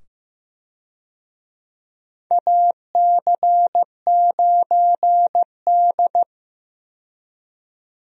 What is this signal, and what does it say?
Morse code, heard as a keyed tone, about 15 wpm, 700 Hz. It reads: AC9D
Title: USA Call Signs - 4 Characters - 15wpm